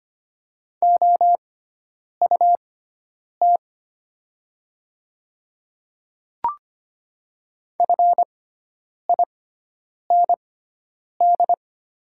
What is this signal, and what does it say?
Morse code, heard as a keyed tone, about 25 wpm, spaced as for 9 wpm, 700 Hz. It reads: OUT FIND
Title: 6x Character Spacing - Top 100 Words - 25wpm